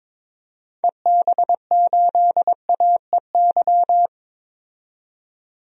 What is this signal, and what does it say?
Morse code, heard as a keyed tone, about 22 wpm, 700 Hz. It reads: EB8AEY
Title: DXCC-100 Call Signs - 6 Characters - 22wpm